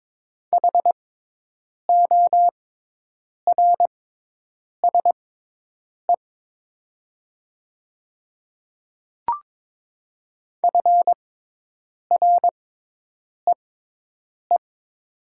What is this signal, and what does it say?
Morse code, heard as a keyed tone, about 22 wpm, spaced as for 8 wpm, 700 Hz. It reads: HORSE FREE
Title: 6x Character Spacing - Top 500 Words - 22wpm